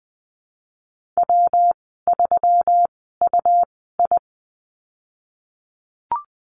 Morse code, heard as a keyed tone, about 20 wpm, spaced as for 15 wpm, 700 Hz. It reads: W3UI
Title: 128 Lesson 21g - Number 6 - Call Signs 20wpm